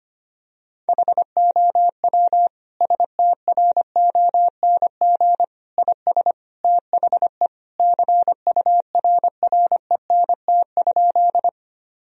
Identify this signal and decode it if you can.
Morse code, heard as a keyed tone, about 25 wpm, 700 Hz. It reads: HOW STRONG IS THE CURRENT?
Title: Sentences from Top 1000 Words - Review Part 4 - 25wpm